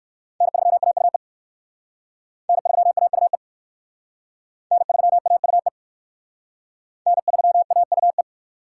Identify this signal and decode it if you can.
Morse code, heard as a keyed tone, about 61 wpm, 700 Hz. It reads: N3AUE N3AUE N3AUE N3AUE